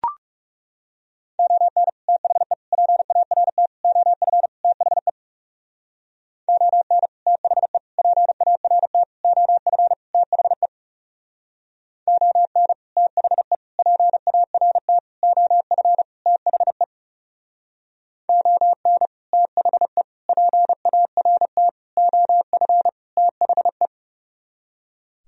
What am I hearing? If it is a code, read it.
Morse code, heard as a keyed tone, about 37 wpm, 700 Hz. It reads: ON THE PART OF THE ON THE PART OF THE ON THE PART OF THE ON THE PART OF THE
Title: Speed-Racing - Sets of 5 Words 30wpm